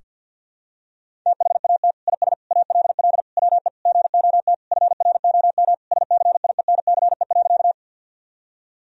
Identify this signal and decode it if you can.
Morse code, heard as a keyed tone, about 50 wpm, 700 Hz. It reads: THAT IS ALL WE GOT FROM SCIENCE.
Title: Sentences from Top 1000 Words - Review Part 2 - 50wpm